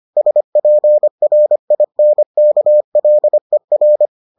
Morse code, heard as a keyed tone, about 25 wpm, 600 Hz. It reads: SPRINKLER